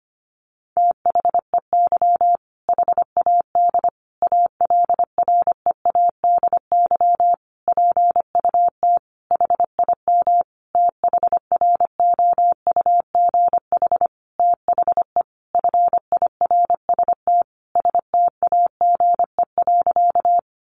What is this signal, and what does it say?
Morse code, heard as a keyed tone, about 25 wpm, 700 Hz. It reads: THEY HAD ALREADY PUT HIM THROUGH THE FIRST STAGE.
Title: Sentences from Top 1200 Words 25wpm